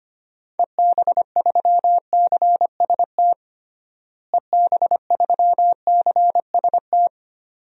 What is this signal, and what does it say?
Morse code, heard as a keyed tone, about 25 wpm, 700 Hz. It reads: EB3CST EB3CST
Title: DXCC-100 Call Signs - 6 Characters - Repeated 2x - 25wpm